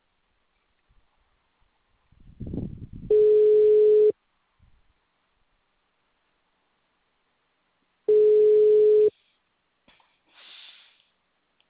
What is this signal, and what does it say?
Ringback tone (Europe, 425 Hz).